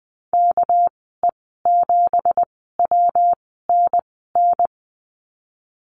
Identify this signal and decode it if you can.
Morse code, heard as a keyed tone, about 20 wpm, spaced as for 15 wpm, 700 Hz. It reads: KE7WNN
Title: 219 Lesson 35g - Number 0 - Call Signs 20wpm